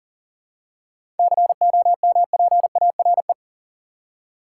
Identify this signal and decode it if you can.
Morse code, heard as a keyed tone, about 40 wpm, 700 Hz. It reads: COMPARE
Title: Seven Letter Words - 40wpm